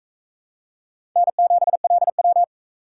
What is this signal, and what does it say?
Morse code, heard as a keyed tone, about 43 wpm, 700 Hz. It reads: N7LW